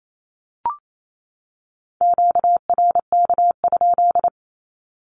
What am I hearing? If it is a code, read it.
Morse code, heard as a keyed tone, about 28 wpm, 700 Hz. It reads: QRK?
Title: Q-Codes 28wpm